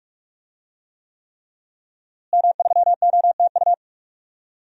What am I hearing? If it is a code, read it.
Morse code, heard as a keyed tone, about 41 wpm, 700 Hz. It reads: M3OTU
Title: DXCC-100 Call Signs - 5 Characters - Repeated 3x - 45wpm